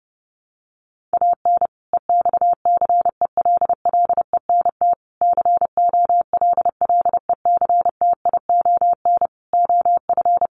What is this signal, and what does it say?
Morse code, heard as a keyed tone, about 30 wpm, 700 Hz. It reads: AN EXCELLENT COLLECTION OF
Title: Sentences from Top 1200 Words 30wpm